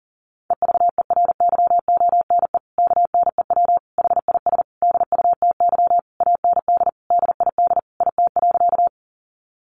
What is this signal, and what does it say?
Morse code, heard as a keyed tone, about 40 wpm, 700 Hz. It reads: EVERYONE KNEW HIS DUTY AND DID IT.
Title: Sentences from Top 1500 Words 40wpm